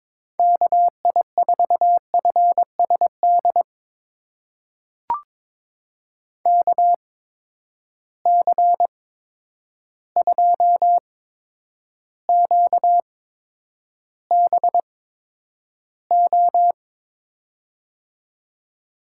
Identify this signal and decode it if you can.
Morse code, heard as a keyed tone, about 22 wpm, spaced as for 9 wpm, 700 Hz. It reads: KI4FSD KC2QBO
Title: USA Call Signs - 6 Characters - 8x Character Spacing - 22wpm